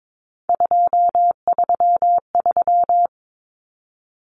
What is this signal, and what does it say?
Morse code, heard as a keyed tone, about 22 wpm, 700 Hz. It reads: 233
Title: Common Cut Numbers - 3-Digit 22wpm